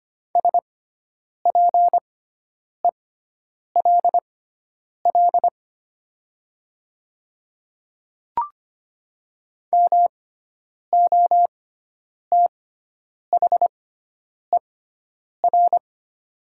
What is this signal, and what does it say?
Morse code, heard as a keyed tone, about 25 wpm, spaced as for 9 wpm, 700 Hz. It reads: SPELL MOTHER
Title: 6x Character Spacing - Top 200 Words - 25wpm